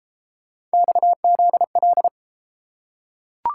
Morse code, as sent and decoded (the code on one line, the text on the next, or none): -..- --.. .-..
XZL